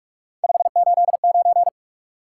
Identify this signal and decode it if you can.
Morse code, heard as a keyed tone, about 42 wpm, 700 Hz. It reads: 589
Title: Top 180 QSO Elements 45wpm